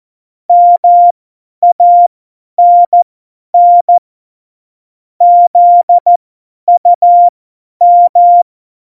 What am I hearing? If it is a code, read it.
Morse code, heard as a keyed tone, about 14 wpm, spaced as for 10 wpm, 700 Hz. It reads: MANN ZUM
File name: Morsekurs / Witze / Witz 09. Januar 2018, 50 F70